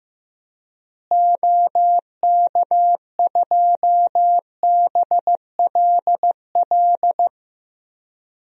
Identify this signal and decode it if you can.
Morse code, heard as a keyed tone, about 16 wpm, 700 Hz. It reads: OK2BLL
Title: DXCC-100 Call Signs - 6 Characters - Repeated 3x - 15wpm